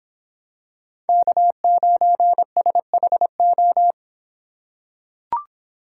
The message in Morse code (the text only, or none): K9SHO